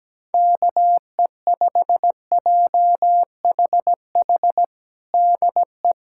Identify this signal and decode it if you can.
Morse code, heard as a keyed tone, about 17 wpm, 700 Hz. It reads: KE5JHH DE
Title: USA Callsign DE Callsign - 6 Characters - 17wpm